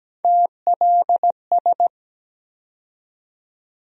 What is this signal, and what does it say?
Morse code, heard as a keyed tone, about 17 wpm, 700 Hz. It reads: TLS
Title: USA Call Sign Postfixes - 3 Letters 17wpm